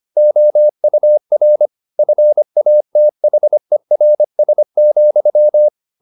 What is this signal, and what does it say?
Morse code, heard as a keyed tone, about 25 wpm, 600 Hz. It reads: OUR FATHERS,
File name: the old man and the sea part 1 25wpm